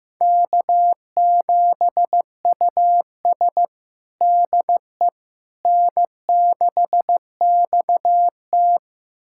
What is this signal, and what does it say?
Morse code, heard as a keyed tone, about 16 wpm, 700 Hz. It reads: K7US DE N6XT
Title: USA Callsign DE Callsign - 4 Characters - 15wpm